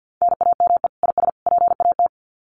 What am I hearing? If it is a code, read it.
Morse code, heard as a keyed tone, about 50 wpm, 700 Hz. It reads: NAME IS PAT
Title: Common QSO Phrases 50wpm